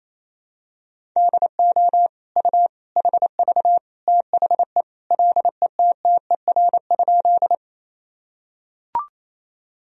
Morse code, heard as a keyed tone, about 28 wpm, 700 Hz. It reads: DO U HV THE LETTER?
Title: Sentences from Top 300 Words and Abbreviations 28wpm